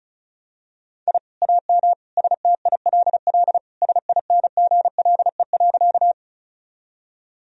Morse code, heard as a keyed tone, about 36 wpm, 700 Hz. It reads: I AM STILL SINGLE.
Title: Sentences from Top 1000 Words - Review Part 4 - 35wpm